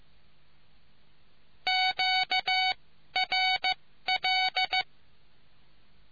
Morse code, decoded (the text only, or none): QRL